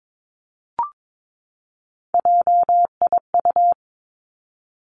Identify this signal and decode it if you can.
Morse code, heard as a keyed tone, about 22 wpm, 700 Hz. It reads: JIU
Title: Three Challenging Letter-Number Combinations 22wpm